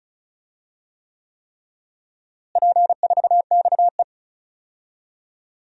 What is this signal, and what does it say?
Morse code, heard as a keyed tone, about 36 wpm, 700 Hz. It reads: P4XE
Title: DXCC-100 Call Signs - 4 Characters - Repeated 3x - 35wpm